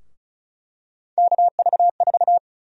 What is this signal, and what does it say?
Morse code, heard as a keyed tone, about 34 wpm, 700 Hz. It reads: KV4
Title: USA Call Sign Prefixes - 35wpm